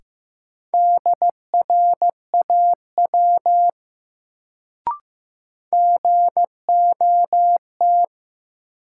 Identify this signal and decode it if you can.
Morse code, heard as a keyed tone, about 16 wpm, 700 Hz. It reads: DRAW GOT
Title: Sentences from Top 300 Words - New Word Drill - 15wpm